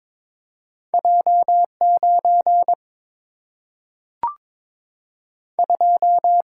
Morse code, heard as a keyed tone, about 22 wpm, 700 Hz. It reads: J9 2
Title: Single-Letter Single-Number 22wpm